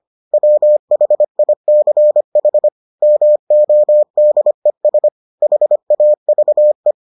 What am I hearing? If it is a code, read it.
Morse code, heard as a keyed tone, about 25 wpm, 600 Hz. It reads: WHICH MODES HAVE